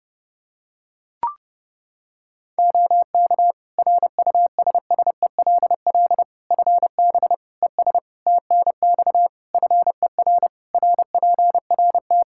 Morse code, heard as a keyed tone, about 30 wpm, 700 Hz. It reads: OK RUSSELL FB ES TNX FER RPRT
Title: Practice QSO - Exchange 2 - Info - 30wpm